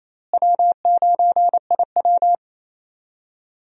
Morse code, heard as a keyed tone, about 28 wpm, 700 Hz. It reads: W9IW
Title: USA Call Signs - 4 Characters - Repeated 2x - 28wpm